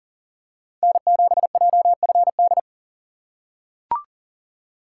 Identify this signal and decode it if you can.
Morse code, heard as a keyed tone, about 40 wpm, 700 Hz. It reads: N7JFD
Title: USA Call Signs - 5 Characters - 2x Character Spacing - 40wpm